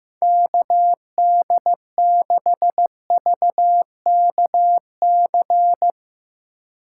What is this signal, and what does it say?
Morse code, heard as a keyed tone, about 16 wpm, 700 Hz. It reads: KD6VKC
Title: Speed Racing - USA 2x3 Call Signs 15wpm